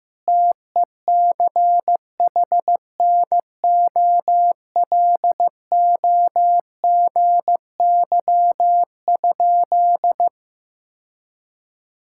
Morse code, heard as a keyed tone, about 15 wpm, 700 Hz. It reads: TECHNOLOGY?
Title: Sentences from Top 1000 Words - Review Part 5 - 15wpm